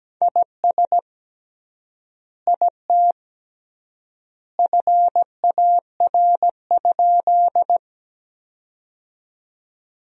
Morse code, heard as a keyed tone, about 17 wpm, 700 Hz. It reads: IS IT FAR?
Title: 3x Word-Spacing - Sentences from Top 300 Words - 17wpm